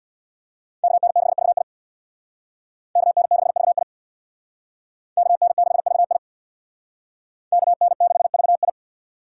Morse code, heard as a keyed tone, about 61 wpm, 700 Hz. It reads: KN6VI KN6VI KN6VI KN6VI
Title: Speed Racing - USA 2x2 Call Signs 50wpm